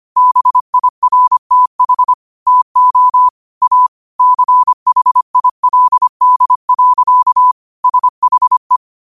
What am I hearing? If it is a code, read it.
Morse code, heard as a keyed tone, about 25 wpm, 1000 Hz. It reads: BIRTH TO A CHILD. SHE
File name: qotdfiles / 25wpm